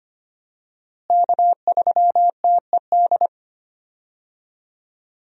Morse code, heard as a keyed tone, about 25 wpm, 700 Hz. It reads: K3TED
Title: USA Call Signs - 5 Characters - Repeated 2x - 25wpm